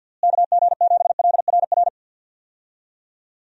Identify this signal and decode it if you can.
Morse code, heard as a keyed tone, about 47 wpm, 700 Hz. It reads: KG7LRR
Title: USA Call Signs - 6 Characters - Repeated 3x - 50wpm